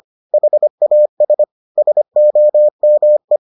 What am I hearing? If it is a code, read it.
Morse code, heard as a keyed tone, about 25 wpm, 600 Hz. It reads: HAS SOME